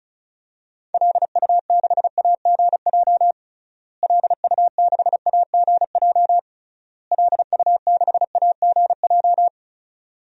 Morse code, heard as a keyed tone, about 36 wpm, 700 Hz. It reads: LU6AGJ LU6AGJ LU6AGJ
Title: DXCC-100 Call Signs - 6 Characters - Repeated 3x - 35wpm